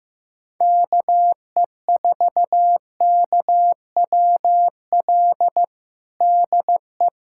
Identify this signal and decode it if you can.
Morse code, heard as a keyed tone, about 16 wpm, 700 Hz. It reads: KE4KWL DE
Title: USA Callsign DE Callsign - 6 Characters - 15wpm